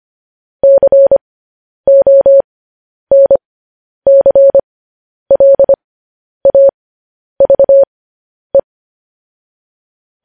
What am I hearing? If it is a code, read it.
Morse code, heard as a keyed tone, about 25 wpm, spaced as for 10 wpm, 550 Hz. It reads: CONCLAVE
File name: podcast / News headlines-1x-25x10